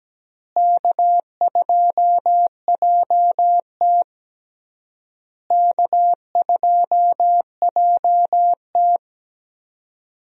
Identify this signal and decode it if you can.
Morse code, heard as a keyed tone, about 17 wpm, 700 Hz. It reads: K2JT K2JT
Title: USA Call Signs - 4 Characters - Repeated 3x - 17wpm